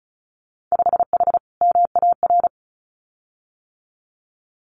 Morse code, heard as a keyed tone, about 36 wpm, 700 Hz. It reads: 5H MAR
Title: Winter Field Day Exchanges Encore 35wpm